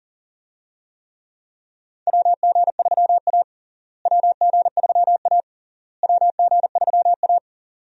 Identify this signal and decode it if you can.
Morse code, heard as a keyed tone, about 40 wpm, 700 Hz. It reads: WG3A WG3A WG3A
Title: USA Call Signs - 4 Characters - Repeated 3x - 40wpm